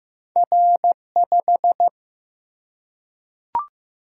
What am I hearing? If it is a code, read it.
Morse code, heard as a keyed tone, about 15 wpm, 700 Hz. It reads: R 5
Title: Single-Letter Single-Number 15wpm